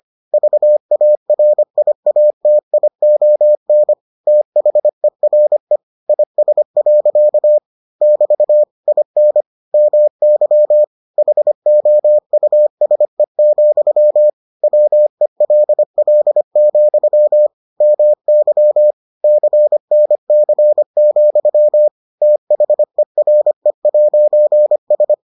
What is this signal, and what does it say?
Morse code, heard as a keyed tone, about 25 wpm, 600 Hz. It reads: VARIATION THERE IS. <BT> IN MY HOUSE, WELL, MY CNC, THERE'S